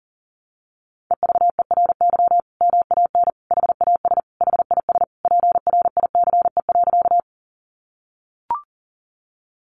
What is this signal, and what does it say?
Morse code, heard as a keyed tone, about 40 wpm, 700 Hz. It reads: EVERY MAN HAS HIS PRICE.
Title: Sentences from Top 600 Words 40wpm